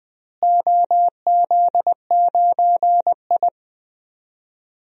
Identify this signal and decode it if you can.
Morse code, heard as a keyed tone, about 21 wpm, 700 Hz. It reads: OZ9I
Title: DXCC-100 Call Signs - 4 Characters - Repeated 2x - 20wpm